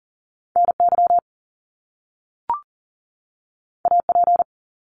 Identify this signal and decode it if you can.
Morse code, heard as a keyed tone, about 40 wpm, 700 Hz. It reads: NY AP
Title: Top Two Letter Combinations in Words 40wpm